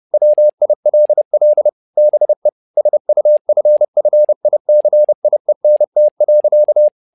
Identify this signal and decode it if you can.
Morse code, heard as a keyed tone, about 30 wpm, 600 Hz. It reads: WILL BE SUFFICIENT.
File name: learning center / tower trouble / tower trouble 30wpm